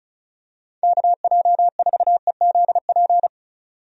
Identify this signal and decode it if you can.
Morse code, heard as a keyed tone, about 36 wpm, 700 Hz. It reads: KJ4EZP
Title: USA Call Signs - 6 Characters - 35wpm